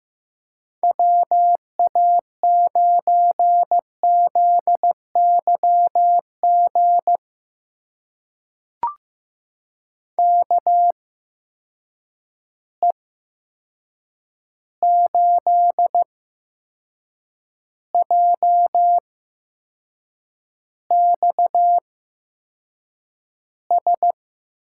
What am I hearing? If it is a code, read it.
Morse code, heard as a keyed tone, about 15 wpm, spaced as for 6 wpm, 700 Hz. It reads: WA9ZYG KE8JXS
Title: USA Call Signs - 8x Character Spacing - 15wpm